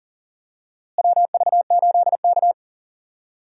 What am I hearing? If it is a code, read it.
Morse code, heard as a keyed tone, about 39 wpm, 700 Hz. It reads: WV8K
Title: USA Call Signs - 4 Characters - Repeated 3x - 40wpm